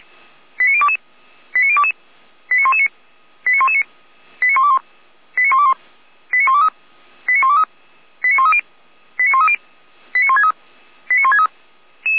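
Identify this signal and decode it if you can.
Five-tone selective call (ZVEI), heard as quick runs of five tones, tones of about 70 ms. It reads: 89022 89022 89119 89119 89121 89121 89123 89123 89130 89130 89163 89163
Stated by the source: Five-tone ZVEI selective call 89017 received via SDR.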